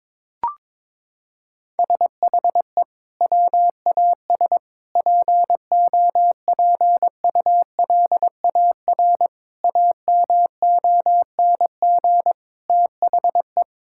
Morse code, heard as a keyed tone, about 22 wpm, 700 Hz. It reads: SHE WAS POPULAR AMONG THE
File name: practice / 22wpm / Sentences with 2-Word Phrases - 22wpm